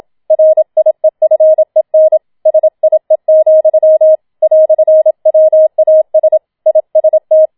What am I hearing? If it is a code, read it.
Morse code, heard as a keyed tone, about 27 wpm, 630 Hz. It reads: RIEFEN SIE, "WAS IST
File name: Schneewittchen Teil2 27wpm